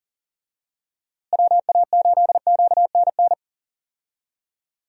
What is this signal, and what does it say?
Morse code, heard as a keyed tone, about 40 wpm, 700 Hz. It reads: WA8QNN
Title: USA Call Signs - 6 Characters - Repeated 3x - 40wpm